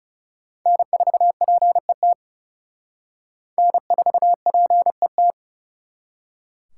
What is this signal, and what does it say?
Morse code, heard as a keyed tone, about 37 wpm, 700 Hz. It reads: N4PET N4PET
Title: Speed Racing - USA 1x3 Call Signs 30wpm